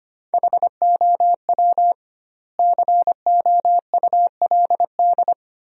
Morse code, heard as a keyed tone, about 25 wpm, 700 Hz. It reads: HOW COULD
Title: 2x Word-Spacing - Sentences from Top 200 Words - 25wpm